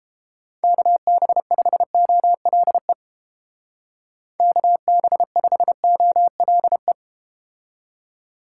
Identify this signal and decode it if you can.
Morse code, heard as a keyed tone, about 31 wpm, 700 Hz. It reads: KB5OLE KB5OLE